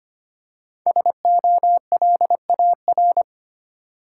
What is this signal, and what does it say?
Morse code, heard as a keyed tone, about 25 wpm, 700 Hz. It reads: SOLAR
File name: 5 Letter Words - First 3 Letters Given - 25wpm